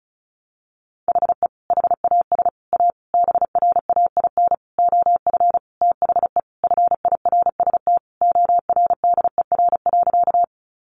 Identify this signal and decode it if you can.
Morse code, heard as a keyed tone, about 34 wpm, 700 Hz. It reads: HE HAS A BRAIN OF THE FIRST ORDER.
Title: Sentences from Top 1100 Words 35wpm